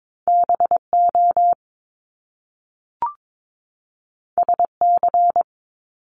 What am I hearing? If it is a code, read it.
Morse code, heard as a keyed tone, about 22 wpm, 700 Hz. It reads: BO SC
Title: Top Two Letter Combinations in Words 22wpm